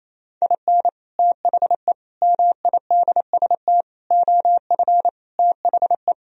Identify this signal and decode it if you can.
Morse code, heard as a keyed tone, about 34 wpm, 700 Hz. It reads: IN THE MIDST OF THE
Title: Speed-Racing - Sets of 5 Words 28wpm